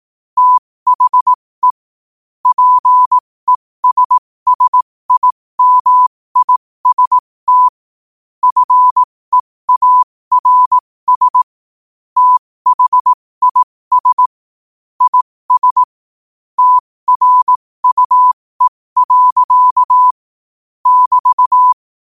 Morse code, heard as a keyed tone, about 18 wpm, spaced as for 15 wpm, 1000 Hz. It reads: THE PESSIMIST FEARS THIS IS TRUE. <BT>
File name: qotdfiles / 15wpm